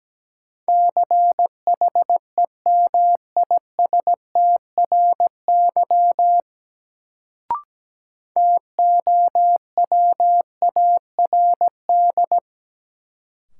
Morse code, heard as a keyed tone, about 17 wpm, 700 Hz. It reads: CHEMISTRY TOWARD
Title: Sentences from Top 1000 Words - Review Part 6 - New Word Drill - 17wpm